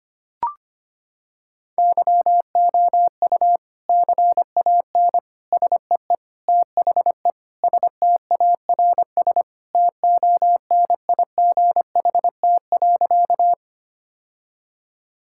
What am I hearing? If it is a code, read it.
Morse code, heard as a keyed tone, about 25 wpm, 700 Hz. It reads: YOU CAN SEE THE STARS TONIGHT.
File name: practice / 25wpm / Sentences with 3-Word Phrases - 25wpm